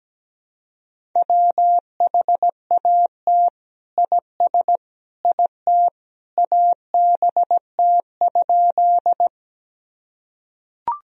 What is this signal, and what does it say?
Morse code, heard as a keyed tone, about 17 wpm, 700 Hz. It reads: WHAT IS IT ABT?
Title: Sentences from Top 100 Words and Abbreviations 17wpm